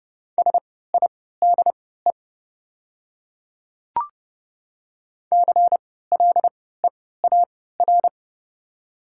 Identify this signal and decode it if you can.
Morse code, heard as a keyed tone, about 30 wpm, spaced as for 17 wpm, 690 Hz. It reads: SIDE CLEAR
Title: 3x Character Spacing - Top 500 Words - 30wpm